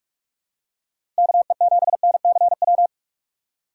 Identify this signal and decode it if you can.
Morse code, heard as a keyed tone, about 46 wpm, 700 Hz. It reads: KE7NCW
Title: USA Call Signs - 6 Characters - 2x Character Spacing - 45wpm